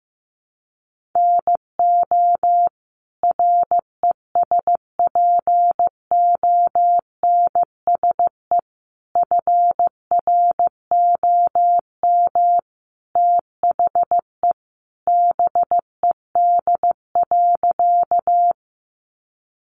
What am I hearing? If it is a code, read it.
Morse code, heard as a keyed tone, about 15 wpm, 700 Hz. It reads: NO RESPONSE FROM THE BED.
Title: Sentences from Top 1400 Words 15wpm